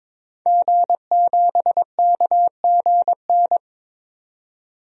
Morse code, heard as a keyed tone, about 22 wpm, 700 Hz. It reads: G7KGN
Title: DXCC-100 Call Signs - 5 Characters - 22wpm